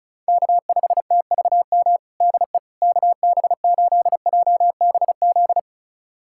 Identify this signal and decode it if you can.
Morse code, heard as a keyed tone, about 34 wpm, 700 Hz. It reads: K5TVM DE KB8JBZ
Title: USA Callsign DE Callsign 35wpm